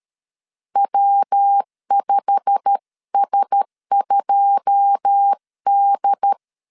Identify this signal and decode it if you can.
Morse code, heard as a keyed tone, about 13 wpm, 800 Hz. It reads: W5S2D
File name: CW / DET / 1 9 1500C S A 12wpm 3800